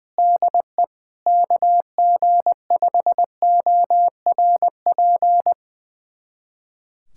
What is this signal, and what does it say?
Morse code, heard as a keyed tone, about 20 wpm, 700 Hz. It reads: DE KG5ORP
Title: USA Callsign DE Callsign - 6 Characters - 20wpm